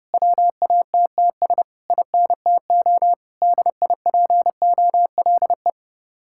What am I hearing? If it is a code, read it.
Morse code, heard as a keyed tone, about 28 wpm, 700 Hz. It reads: WATTS INTO DIPOLE